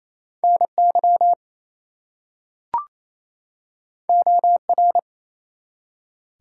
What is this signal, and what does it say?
Morse code, heard as a keyed tone, about 28 wpm, 700 Hz. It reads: NY OR
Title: US States and Canadian Provinces 28wpm